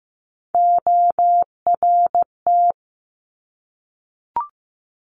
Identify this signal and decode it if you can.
Morse code, heard as a keyed tone, about 15 wpm, 700 Hz. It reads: ORT
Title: Top Three Letter Combinations in Words 15wpm